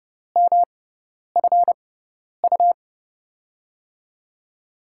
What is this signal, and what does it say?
Morse code, heard as a keyed tone, about 30 wpm, spaced as for 10 wpm, 700 Hz. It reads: MFU
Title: USA Call Signs - 6 Characters - 6x Character Spacing - 30wpm